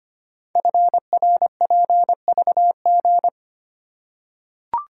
Morse code, heard as a keyed tone, about 25 wpm, 700 Hz. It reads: FRPVG
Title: Code Groups - 5 Letters - 25wpm